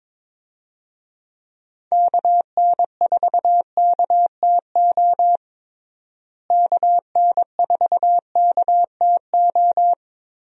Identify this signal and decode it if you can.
Morse code, heard as a keyed tone, about 22 wpm, 700 Hz. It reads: KN4KTO KN4KTO
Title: USA Call Signs - 6 Characters - Repeated 3x - 22wpm